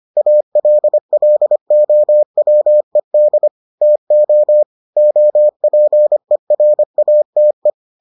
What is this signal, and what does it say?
Morse code, heard as a keyed tone, about 25 wpm, 600 Hz. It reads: ALLOWED TO OPERATE